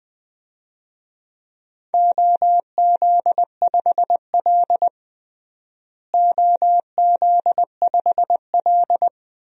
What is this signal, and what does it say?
Morse code, heard as a keyed tone, about 20 wpm, 700 Hz. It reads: OZ5L OZ5L
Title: DXCC-100 Call Signs - 4 Characters - Repeated 3x - 20wpm